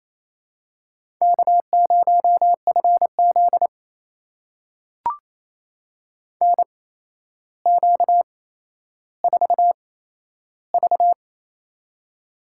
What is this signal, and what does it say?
Morse code, heard as a keyed tone, about 28 wpm, spaced as for 12 wpm, 700 Hz. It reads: K0FZ NQ4V
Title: USA Call Signs - 4 Characters - 8x Character Spacing - 28wpm